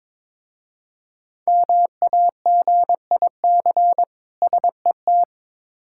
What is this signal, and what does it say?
Morse code, heard as a keyed tone, about 22 wpm, 700 Hz. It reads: MAGIC SET
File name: Difficult Words with Hints - 22wpm